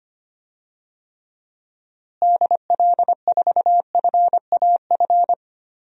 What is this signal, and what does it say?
Morse code, heard as a keyed tone, about 25 wpm, 700 Hz. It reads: DL4FAF
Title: DXCC-100 Call Signs - 6 Characters - Repeated 3x - 25wpm